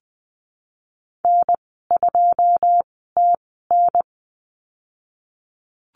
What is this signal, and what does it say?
Morse code, heard as a keyed tone, about 20 wpm, spaced as for 14 wpm, 700 Hz. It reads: N2TN
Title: 122 Lesson 20g - Number 3 - Call Signs 20wpm